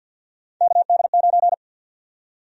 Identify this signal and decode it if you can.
Morse code, heard as a keyed tone, about 49 wpm, 700 Hz. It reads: KD9
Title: USA Call Sign Prefixes - Repeated 2x - 50wpm